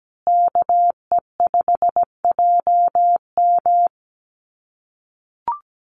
Morse code, heard as a keyed tone, about 17 wpm, 700 Hz. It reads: KE5JM
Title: USA Call Signs Repeated 2x 17wpm